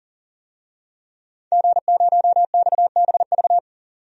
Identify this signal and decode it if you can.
Morse code, heard as a keyed tone, about 40 wpm, 700 Hz. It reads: G0XBV
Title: DXCC-100 Call Signs - 5 Characters - 40wpm